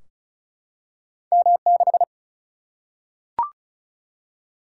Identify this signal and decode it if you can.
Morse code, heard as a keyed tone, about 36 wpm, 700 Hz. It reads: M6
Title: Single-Letter Single-Number 35wpm